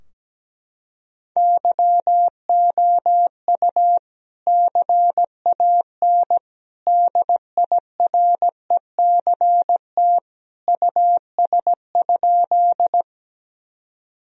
Morse code, heard as a keyed tone, about 17 wpm, 700 Hz. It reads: YOU CAN DIRECT US?
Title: Sentences from Top 1000 Words - Review Part 2 - 17wpm